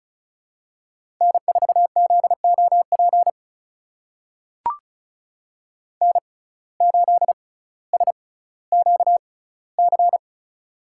Morse code, heard as a keyed tone, about 36 wpm, spaced as for 18 wpm, 700 Hz. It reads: N4ZOP N8SQC
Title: USA Call Signs - 5 Characters - 6x Character Spacing - 35wpm